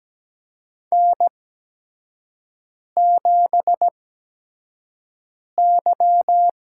Morse code, heard as a keyed tone, about 17 wpm, spaced as for 5 wpm, 700 Hz. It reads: N7Y
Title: USA Call Signs - 5 Characters - 8x Character Spacing - 17wpm